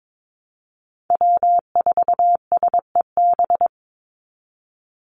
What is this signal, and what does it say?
Morse code, heard as a keyed tone, about 22 wpm, 700 Hz. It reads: W4SEB
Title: USA Call Signs 22wpm